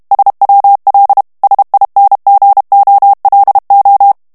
Morse code, heard as a keyed tone, about 32 wpm, 800 Hz. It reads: SWL SINGOLO
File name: texte clair 2 vitesse 30 mots minute